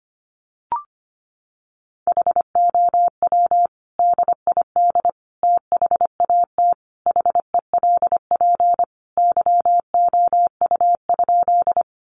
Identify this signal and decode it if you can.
Morse code, heard as a keyed tone, about 25 wpm, 700 Hz. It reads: HOW DID THAT HELP YOU?
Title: Sentences from Top 200 Words 25wpm